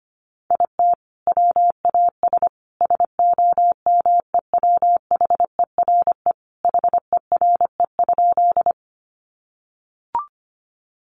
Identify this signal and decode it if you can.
Morse code, heard as a keyed tone, about 25 wpm, 700 Hz. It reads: IT WAS SOMEWHERE HERE?
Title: Sentences from Top 1400 Words 25wpm